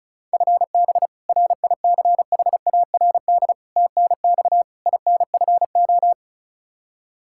Morse code, heard as a keyed tone, about 36 wpm, 700 Hz. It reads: FB RICHARD TNX INFO
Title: Practice QSO - Exchange 3 - More Info - 35wpm